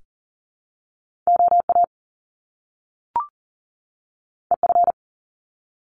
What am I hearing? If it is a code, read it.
Morse code, heard as a keyed tone, about 40 wpm, 700 Hz. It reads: OA EF
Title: Top Two Letter Combinations in Words 40wpm